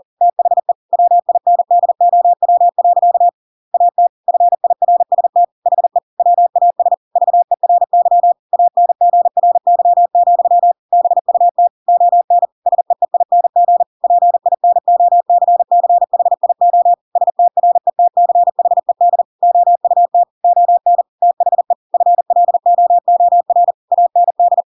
Morse code, heard as a keyed tone, about 40 wpm, 700 Hz. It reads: THE WINDOW. AT FIRST HE WAS VERY ANGRY, BUT ON SEEING PINOCCHIO STRETCHED OUT ON THE FLOOR AND
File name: Pinocchio / Pinocchio ch7 40 40 ik1hge